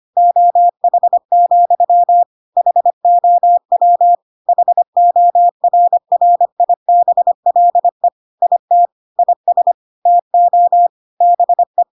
Morse code, heard as a keyed tone, about 25 wpm, 700 Hz. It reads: OH, HOW HORRIBLE IT IS TO BE
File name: Pinocchio / Pinocchio ch5 25 25 ik1hge